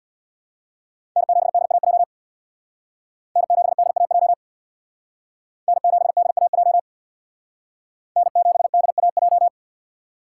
Morse code, heard as a keyed tone, about 63 wpm, 700 Hz. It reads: N7DAJ N7DAJ N7DAJ N7DAJ